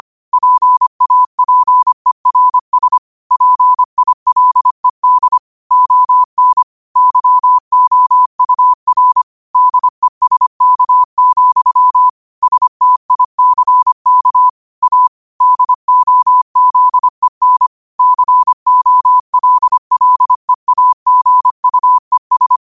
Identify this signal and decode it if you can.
Morse code, heard as a keyed tone, about 25 wpm, 1000 Hz. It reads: PAPERS PILED ON YOUR DESK, STICK A DOZEN COLLEAGUES
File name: qotdfiles / 25wpm